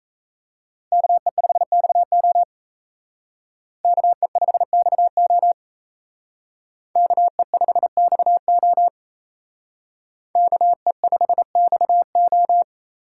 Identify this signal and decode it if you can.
Morse code, heard as a keyed tone, about 34 wpm, 700 Hz. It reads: KE5XO KE5XO KE5XO KE5XO
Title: Speed Racing - USA 2x2 Call Signs 28wpm